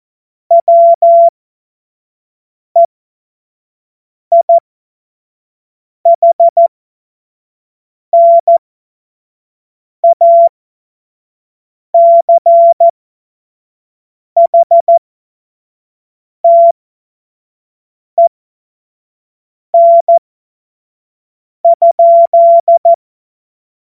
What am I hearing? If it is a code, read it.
Morse code, heard as a keyed tone, about 14 wpm, spaced as for 5 wpm, 680 Hz. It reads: WEIHNACHTEN?
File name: Morsekurs / Witze / Witz 13. Dezember 2022 25 F70